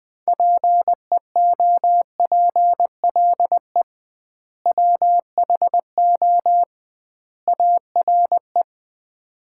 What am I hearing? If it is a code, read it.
Morse code, heard as a keyed tone, about 20 wpm, 700 Hz. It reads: PEOPLE WHO ARE
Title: 2x Word-Spacing - Sets of 3 Words - 20wpm